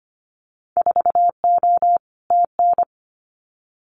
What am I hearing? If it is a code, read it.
Morse code, heard as a keyed tone, about 25 wpm, 700 Hz. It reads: 4O TN
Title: Winter Field Day Exchanges Encore 25wpm